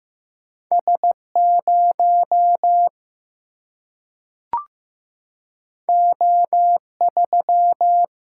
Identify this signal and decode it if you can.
Morse code, heard as a keyed tone, about 15 wpm, 700 Hz. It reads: S0 O3
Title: Single-Letter Single-Number 15wpm